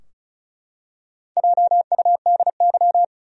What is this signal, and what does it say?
Morse code, heard as a keyed tone, about 36 wpm, 700 Hz. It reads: JUDY
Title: USA Common Names - Max 5 Letters - 35wpm